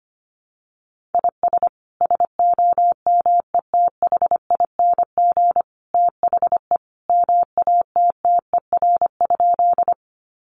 Morse code, heard as a keyed tone, about 25 wpm, 700 Hz. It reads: IS SOMETHING THE MATTER?
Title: Sentences from Top 700 Words 25wpm